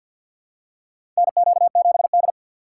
Morse code, heard as a keyed tone, about 53 wpm, 700 Hz. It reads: NQ7D
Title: USA Call Signs - 4 Characters - 6x Character Spacing - 50wpm